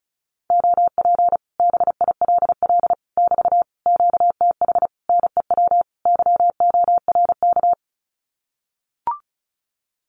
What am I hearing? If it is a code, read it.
Morse code, heard as a keyed tone, about 34 wpm, 700 Hz. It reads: OP BILL <BT> QTH NEW YORK
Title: Common QSO Phrases Encore 35wpm